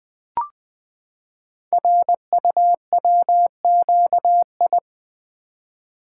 Morse code, heard as a keyed tone, about 20 wpm, 700 Hz. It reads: RUWQI
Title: Code Groups - 5 Letters - 20wpm